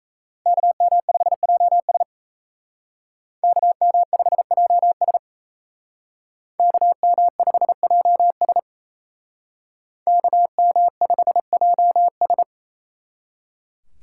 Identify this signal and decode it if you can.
Morse code, heard as a keyed tone, about 34 wpm, 700 Hz. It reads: KM5JS KM5JS KM5JS KM5JS
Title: Speed Racing - USA Call Signs 28wpm